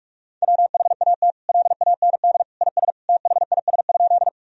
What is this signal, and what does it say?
Morse code, heard as a keyed tone, about 46 wpm, 700 Hz. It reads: WHAT LAND IS THIS?
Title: Sentences from Top 1000 Words - Review Part 1 - 45wpm